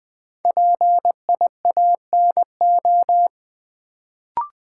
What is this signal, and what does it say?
Morse code, heard as a keyed tone, about 20 wpm, 700 Hz. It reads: PIANO
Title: Build A Word - 1 Letter At A Time - 3 to 5-Letter Words - 20wpm